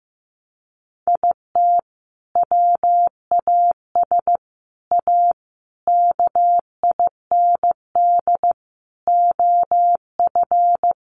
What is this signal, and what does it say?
Morse code, heard as a keyed tone, about 15 wpm, 700 Hz. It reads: IT WAS A KIND OF
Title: Sentences from Top 500 Words 15wpm